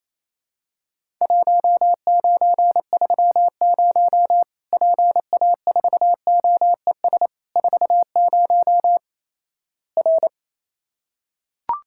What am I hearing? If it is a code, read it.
Morse code, heard as a keyed tone, about 28 wpm, 700 Hz. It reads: 1930 PA4OES 40 R
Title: Worked All Europe DX Contest - QTC Exchange - 28wpm